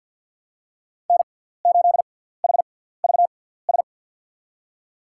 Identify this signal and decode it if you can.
Morse code, heard as a keyed tone, about 47 wpm, spaced as for 17 wpm, 700 Hz. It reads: N8HVS
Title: USA Call Signs - 6x Character Spacing - 50wpm